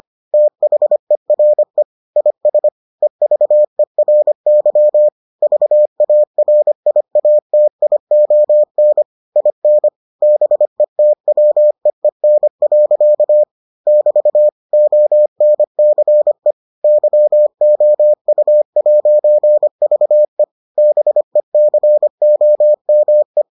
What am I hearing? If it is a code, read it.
Morse code, heard as a keyed tone, about 25 wpm, 600 Hz. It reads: THERE IS EVERY VARIATION IN BETWEEN. <BT> ONCE YOU'VE BECOME